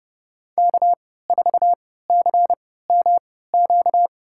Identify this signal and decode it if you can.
Morse code, heard as a keyed tone, about 30 wpm, spaced as for 17 wpm, 700 Hz. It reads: K4CMQ
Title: USA Call Signs - 3x Character Spacing - 30wpm